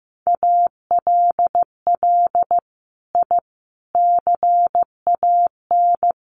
Morse code, heard as a keyed tone, about 15 wpm, 700 Hz. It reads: ALL I CAN
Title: Sets of 5 Words Encore2 15wpm